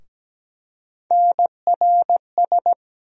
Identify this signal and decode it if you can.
Morse code, heard as a keyed tone, about 17 wpm, 700 Hz. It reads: NRS
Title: USA Call Sign Postfixes - 3 Letters 17wpm